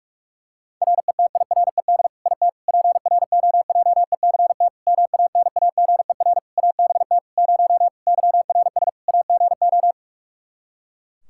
Morse code, heard as a keyed tone, about 46 wpm, 700 Hz. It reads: RETIRED IT PROJECT MANAGER ABT 0 YRS AGO
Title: Practice QSO - Exchange 3 - More Info - 45wpm